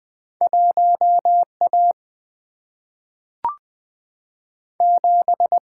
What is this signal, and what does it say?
Morse code, heard as a keyed tone, about 20 wpm, 700 Hz. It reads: 1 A 7
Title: Single-Letter Single-Number 20wpm